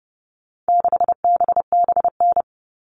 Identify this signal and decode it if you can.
Morse code, heard as a keyed tone, about 30 wpm, 700 Hz. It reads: 6BBN
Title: All Cut Numbers - 4-Digit 30wpm